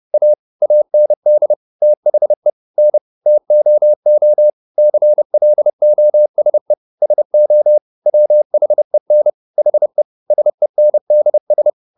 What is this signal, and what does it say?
Morse code, heard as a keyed tone, about 30 wpm, 600 Hz. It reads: A AND THE N TOO CLOSE SO WHEN HE SENDS